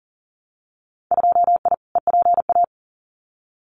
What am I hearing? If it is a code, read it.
Morse code, heard as a keyed tone, about 40 wpm, 700 Hz. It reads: 2I EPA
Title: Winter Field Day Exchanges 40wpm